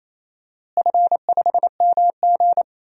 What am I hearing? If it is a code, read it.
Morse code, heard as a keyed tone, about 28 wpm, 700 Hz. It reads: F5MG
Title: DXCC-100 Call Signs - 4 Characters - Repeated 2x - 28wpm